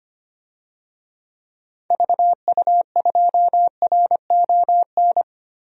Morse code, heard as a keyed tone, about 25 wpm, 700 Hz. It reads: VU2RON
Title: DXCC-100 Call Signs - 6 Characters - 25wpm